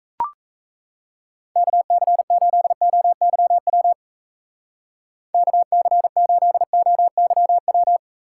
Morse code, heard as a keyed tone, about 34 wpm, 700 Hz. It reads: KC8OYW KC8OYW
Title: Speed Racing - USA 2x3 Call Signs 28wpm